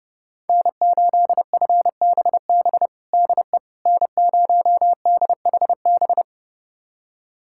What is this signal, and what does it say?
Morse code, heard as a keyed tone, about 30 wpm, 700 Hz. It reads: N8FBB DE N0DHB
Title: USA Callsign DE Callsign - 5 Characters - 30wpm